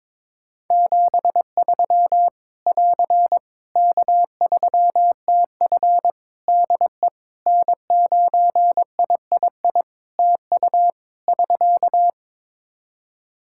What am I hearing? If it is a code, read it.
Morse code, heard as a keyed tone, about 22 wpm, 700 Hz. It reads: 73 <AR> K3TF DE N9III TU <SK>
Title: Practice QSO - Exchange 4 - Ending - 22wpm